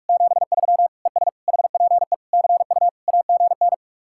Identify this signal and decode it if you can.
Morse code, heard as a keyed tone, about 49 wpm, 700 Hz. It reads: 73 ES HPE CU AGN